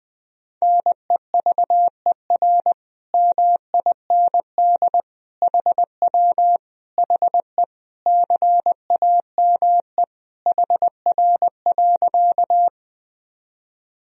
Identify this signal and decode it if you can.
Morse code, heard as a keyed tone, about 21 wpm, 700 Hz. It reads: NEVER MIND HW HE CAME HR.
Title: Sentences from Top 500 Words and Abbreviations 20wpm